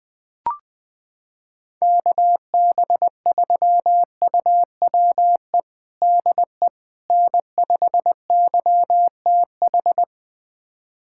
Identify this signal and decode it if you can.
Morse code, heard as a keyed tone, about 20 wpm, 700 Hz. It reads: KB3UWE DE N5YTH
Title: USA Callsign DE Callsign 20wpm